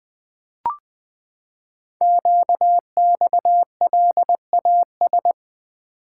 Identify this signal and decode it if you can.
Morse code, heard as a keyed tone, about 20 wpm, 700 Hz. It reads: QXLAS
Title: Code Groups - 5 Letters - 20wpm